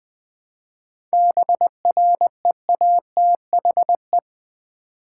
Morse code, heard as a keyed tone, about 20 wpm, 700 Hz. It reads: BREATHE
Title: 7 Letter Words - First 3 Letters Given - 20wpm